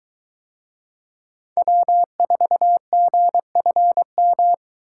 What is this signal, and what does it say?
Morse code, heard as a keyed tone, about 22 wpm, 700 Hz. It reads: W4GFM